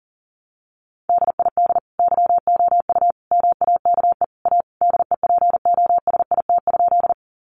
Morse code, heard as a keyed tone, about 40 wpm, 700 Hz. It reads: DID YOU MAKE A DEPOSIT?
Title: Sentences from Top 1300 Words 40wpm